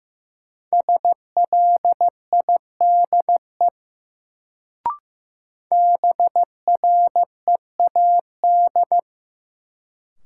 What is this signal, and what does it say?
Morse code, heard as a keyed tone, about 15 wpm, 700 Hz. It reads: SLIDE BREAD
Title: Sentences from Top 1600 Words - New Word Drill - 15wpm